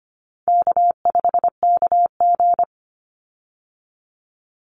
Morse code, heard as a keyed tone, about 25 wpm, 700 Hz. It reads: K5KG
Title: Call Signs - 25wpm